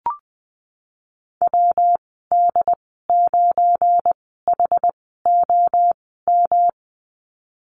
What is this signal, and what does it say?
Morse code, heard as a keyed tone, about 20 wpm, spaced as for 15 wpm, 700 Hz. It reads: WD9HOM
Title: 179 Lesson 29g - Slash - Call Signs 20wpm